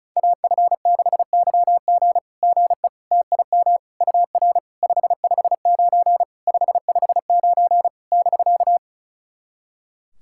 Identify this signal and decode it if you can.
Morse code, heard as a keyed tone, about 36 wpm, 690 Hz. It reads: AF6YG GE TIM UR 559 559 <BK>
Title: SOTA QSO Practice 35wpm